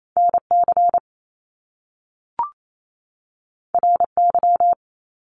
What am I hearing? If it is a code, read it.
Morse code, heard as a keyed tone, about 28 wpm, 700 Hz. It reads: NC RY
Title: Top Two Letter Combinations in Words 28wpm